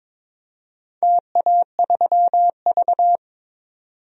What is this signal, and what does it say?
Morse code, heard as a keyed tone, about 22 wpm, 700 Hz. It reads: TA3V
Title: DXCC-100 Call Signs - 4 Characters - 22wpm